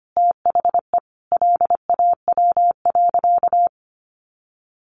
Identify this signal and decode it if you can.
Morse code, heard as a keyed tone, about 25 wpm, 700 Hz. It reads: THE LAW.
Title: Sentences from Top 1400 Words 25wpm